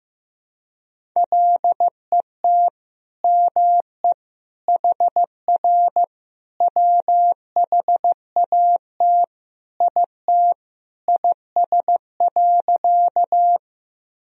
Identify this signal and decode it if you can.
Morse code, heard as a keyed tone, about 16 wpm, 700 Hz. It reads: LET ME HR WHAT IT IS.
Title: Sentences from Top 300 Words and Abbreviations 15wpm